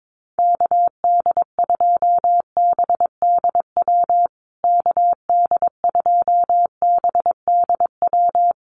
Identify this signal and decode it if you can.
Morse code, heard as a keyed tone, about 22 wpm, 700 Hz. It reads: KD2BDW KD2BDW
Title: USA Call Signs Repeated 3x 22wpm